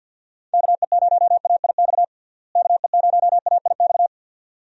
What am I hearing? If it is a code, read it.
Morse code, heard as a keyed tone, about 52 wpm, 700 Hz. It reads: KE0AIX KE0AIX
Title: USA Call Signs - 6 Characters - Repeated 2x - 50wpm